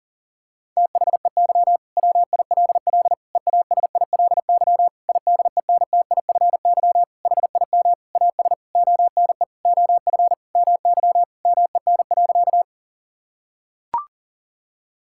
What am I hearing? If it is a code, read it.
Morse code, heard as a keyed tone, about 40 wpm, 700 Hz. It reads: THEY WILL EASILY IDENTIFY HIM AS ONE OF MY MEN.
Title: Sentences from Top 1000 Words - Review Part 6 - 40wpm